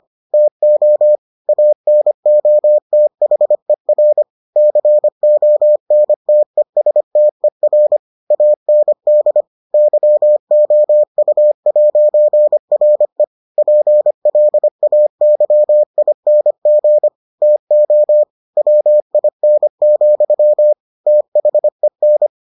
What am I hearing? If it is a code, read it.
Morse code, heard as a keyed tone, about 25 wpm, 600 Hz. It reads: TO ANOTHER CONTESTER AND YOU'RE PLAYING TO WIN, THEN